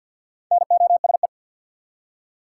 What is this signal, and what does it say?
Morse code, heard as a keyed tone, about 48 wpm, 700 Hz. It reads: NOSE
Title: Four Letter Words - 50wpm